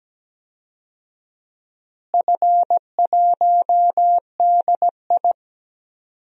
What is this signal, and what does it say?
Morse code, heard as a keyed tone, about 17 wpm, 700 Hz. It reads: F1DI
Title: DXCC-100 Call Signs - 4 Characters - Repeated 3x - 17wpm